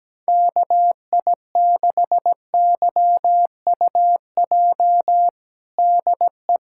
Morse code, heard as a keyed tone, about 17 wpm, 700 Hz. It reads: KI6YUJ DE
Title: USA Callsign DE Callsign 17wpm